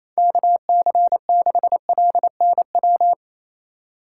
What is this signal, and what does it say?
Morse code, heard as a keyed tone, about 28 wpm, 700 Hz. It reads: KC6LNW
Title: USA Call Signs - 6 Characters - 3x Character Spacing - 28wpm